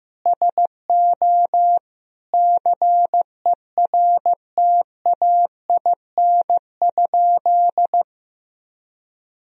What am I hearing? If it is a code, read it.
Morse code, heard as a keyed tone, about 15 wpm, 700 Hz. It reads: SO CERTAIN?
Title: Sentences from Top 1000 Words - Review Part 2 - 15wpm